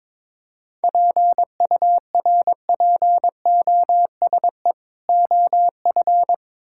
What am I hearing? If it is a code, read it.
Morse code, heard as a keyed tone, about 22 wpm, 700 Hz. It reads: PURPOSE OF
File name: practice / 22wpm / Build A Phrase - 5-Word Phrases - 22wpm